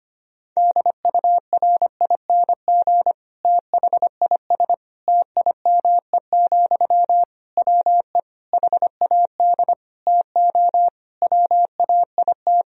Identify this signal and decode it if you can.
Morse code, heard as a keyed tone, about 25 wpm, 700 Hz. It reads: DURING THIS TIME, WE HAD TO WAIT
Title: Sentences from Top 1000 Words - Review Part 2 - 25wpm